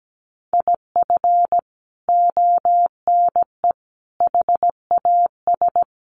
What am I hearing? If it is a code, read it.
Morse code, heard as a keyed tone, about 17 wpm, 700 Hz. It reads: IF ONE HAS
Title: Sentences from Top 1400 Words 17wpm